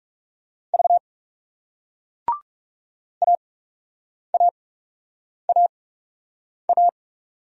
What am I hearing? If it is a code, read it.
Morse code, heard as a keyed tone, about 39 wpm, spaced as for 8 wpm, 700 Hz. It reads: V AAAA